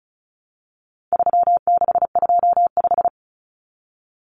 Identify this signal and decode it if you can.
Morse code, heard as a keyed tone, about 34 wpm, 700 Hz. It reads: 3625
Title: Common Cut Numbers - 4-Digit 35wpm